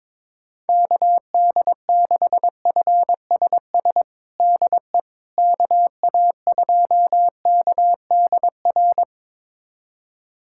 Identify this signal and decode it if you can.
Morse code, heard as a keyed tone, about 22 wpm, 700 Hz. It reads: KD6FSS DE KA2KDR
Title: USA Callsign DE Callsign 22wpm